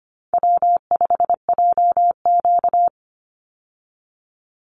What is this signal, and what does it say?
Morse code, heard as a keyed tone, about 25 wpm, 700 Hz. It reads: W5JQ
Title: Call Signs Repeated 2x - 25wpm